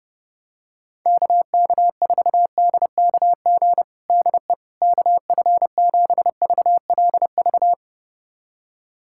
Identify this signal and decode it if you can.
Morse code, heard as a keyed tone, about 30 wpm, 700 Hz. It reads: KK4DKG DE KF7VLV
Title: USA Callsign DE Callsign 30wpm